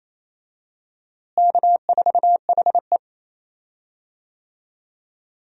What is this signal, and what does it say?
Morse code, heard as a keyed tone, about 28 wpm, 700 Hz. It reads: K4HE
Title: USA Call Signs - 4 Characters - Repeated 2x - 28wpm